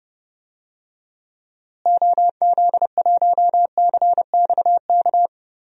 Morse code, heard as a keyed tone, about 31 wpm, 700 Hz. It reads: OZ1CXK
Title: DXCC-100 Call Signs - 6 Characters - Repeated 3x - 30wpm